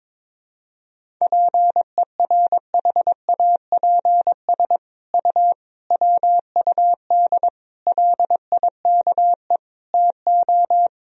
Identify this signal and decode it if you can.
Morse code, heard as a keyed tone, about 21 wpm, 700 Hz. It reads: PERHAPS U WUD LIKE TO
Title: Sentences from Top 500 Words and Abbreviations 22wpm